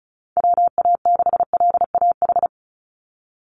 Morse code, heard as a keyed tone, about 34 wpm, 700 Hz. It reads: WA6LAH
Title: USA Call Signs 35wpm